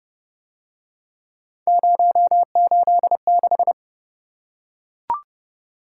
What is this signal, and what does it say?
Morse code, heard as a keyed tone, about 29 wpm, 700 Hz. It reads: 086